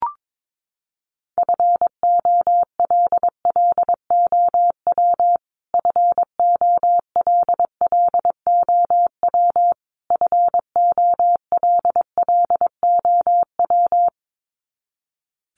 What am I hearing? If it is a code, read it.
Morse code, heard as a keyed tone, about 22 wpm, 700 Hz. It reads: FOLLOW FOLLOW FOLLOW
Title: Top 200 Words Repeated 22wpm